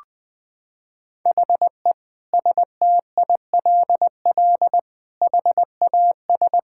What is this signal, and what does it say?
Morse code, heard as a keyed tone, about 20 wpm, 700 Hz. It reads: HE STILL HAS
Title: Sentences with 7-Word Phrases - 20wpm